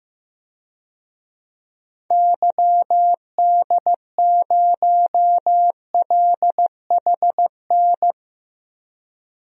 Morse code, heard as a keyed tone, about 15 wpm, 700 Hz. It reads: YD0LHN
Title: DXCC-100 Call Signs - 6 Characters - Repeated 2x - 15wpm